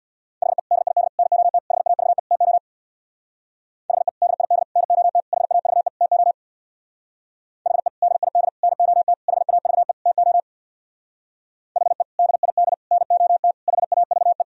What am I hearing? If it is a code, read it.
Morse code, heard as a keyed tone, about 64 wpm, 700 Hz. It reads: HE DID NOT HAVE TO HE DID NOT HAVE TO HE DID NOT HAVE TO HE DID NOT HAVE